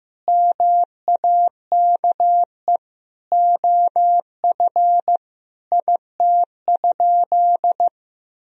Morse code, heard as a keyed tone, about 15 wpm, 700 Hz. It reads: MAKE OF IT?
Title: Sentences from Top 100 Words and Abbreviations 15wpm